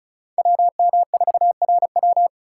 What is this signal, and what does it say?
Morse code, heard as a keyed tone, about 46 wpm, 700 Hz. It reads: WM4RW